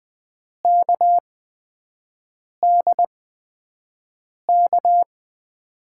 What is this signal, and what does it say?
Morse code, heard as a keyed tone, about 20 wpm, spaced as for 5 wpm, 700 Hz. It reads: KDK
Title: USA Call Signs - 5 Characters - 8x Character Spacing - 20wpm